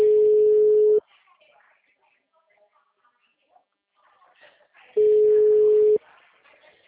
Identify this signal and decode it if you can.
Ringback tone (Europe, 425 Hz), with speech.